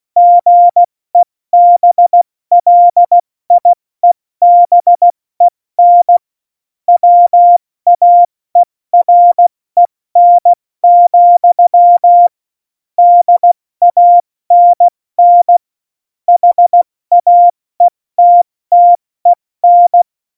Morse code, heard as a keyed tone, about 16 wpm, 700 Hz. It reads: GEBLIEBEN WAEREN, DANN HAETTEN
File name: Morsekurs / Witze / Witz 22. August 17, Tempo 70 F80